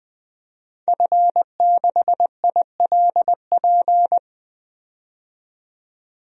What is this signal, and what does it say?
Morse code, heard as a keyed tone, about 20 wpm, 700 Hz. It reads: F6ILP
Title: DXCC-100 Call Signs - 5 Characters - 20wpm